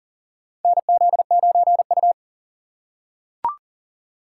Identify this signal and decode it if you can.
Morse code, heard as a keyed tone, about 38 wpm, 700 Hz. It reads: NZ9U
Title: USA Call Signs - 4 Characters - 4x Character Spacing - 40wpm